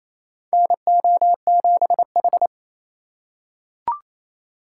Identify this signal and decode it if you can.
Morse code, heard as a keyed tone, about 28 wpm, 700 Hz. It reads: NO7H
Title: USA Call Signs - 2x Character Spacing - 28wpm